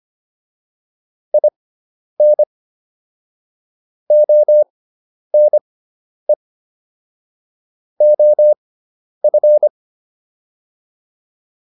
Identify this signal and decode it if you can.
Morse code, heard as a keyed tone, about 25 wpm, spaced as for 10 wpm, 600 Hz. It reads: IN ONE OF